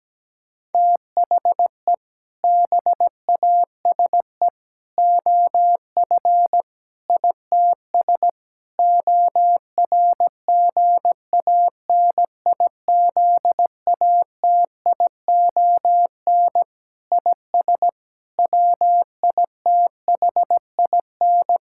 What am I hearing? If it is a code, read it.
Morse code, heard as a keyed tone, about 17 wpm, 700 Hz. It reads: THE BASE OF ITS ORGANIZATION IS WITHIN
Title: Sentences from Top 1000 Words - Review Part 5 - 17wpm